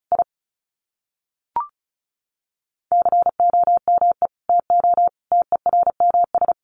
Morse code, heard as a keyed tone, about 34 wpm, 700 Hz. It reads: I COME TO TERMS
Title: Sets of 5 Words Encore2 35wpm